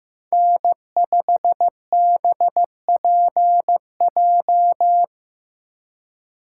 Morse code, heard as a keyed tone, about 16 wpm, 700 Hz. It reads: N5BPJ
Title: USA Call Signs - 5 Characters - 4x Character Spacing - 15wpm